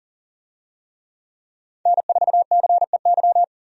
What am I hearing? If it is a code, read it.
Morse code, heard as a keyed tone, about 40 wpm, 700 Hz. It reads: N4CEY
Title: USA Call Signs - 5 Characters - Repeated 2x - 40wpm